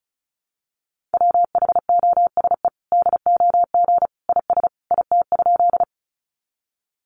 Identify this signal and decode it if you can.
Morse code, heard as a keyed tone, about 34 wpm, 700 Hz. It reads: WHOSE DOG IS IT?
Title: Sentences from Top 1200 Words 35wpm